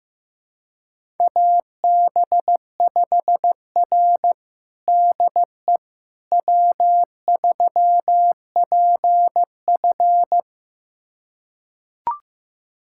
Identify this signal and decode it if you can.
Morse code, heard as a keyed tone, about 15 wpm, 700 Hz. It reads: AB5R DE W3PF
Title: USA Callsign DE Callsign - 4 Characters - 15wpm